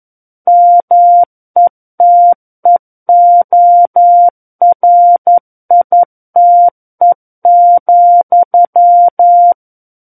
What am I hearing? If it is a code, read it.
Morse code, heard as a keyed tone, about 11 wpm, 700 Hz. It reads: METEORITE,
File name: 11 WPM